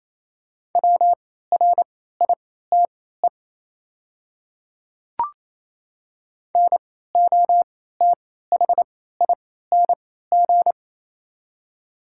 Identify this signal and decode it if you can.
Morse code, heard as a keyed tone, about 28 wpm, spaced as for 16 wpm, 690 Hz. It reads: WRITE NOTHING
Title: 3x Character Spacing - Top 500 Words - 28wpm